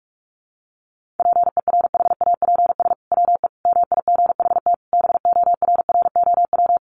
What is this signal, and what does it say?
Morse code, heard as a keyed tone, about 44 wpm, 700 Hz. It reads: PERHAPS WE MIGHT BORROW
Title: Sentences from Top 1500 Words 45wpm